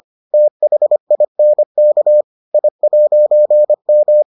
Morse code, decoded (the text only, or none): THINK I'M